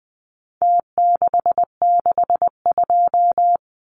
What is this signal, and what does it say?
Morse code, heard as a keyed tone, about 20 wpm, 700 Hz. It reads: T662
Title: Numbers - Contest Style - 4-Digit 20wpm